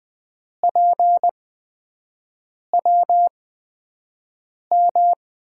Morse code, heard as a keyed tone, about 20 wpm, spaced as for 5 wpm, 700 Hz. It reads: PWM
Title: USA Call Signs - 5 Characters - 8x Character Spacing - 20wpm